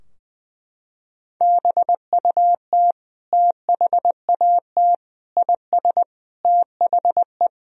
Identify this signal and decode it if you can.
Morse code, heard as a keyed tone, about 21 wpm, 700 Hz. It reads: BUT THAT IS THE
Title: Sentences from Top 1000 Words - Review Part 3 - 20wpm